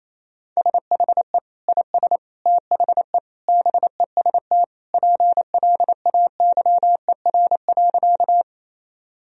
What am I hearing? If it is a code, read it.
Morse code, heard as a keyed tone, about 27 wpm, 700 Hz. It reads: SHE IS THE BEST PLAYER.
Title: Sentences from Top 1000 Words - Review Part 4 - 28wpm